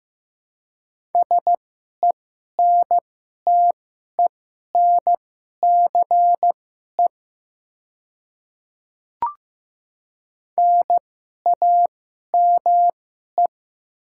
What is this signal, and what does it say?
Morse code, heard as a keyed tone, about 15 wpm, spaced as for 11 wpm, 700 Hz. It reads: SENTENCE NAME
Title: 2x Character Spacing - Top 200 Words - 15wpm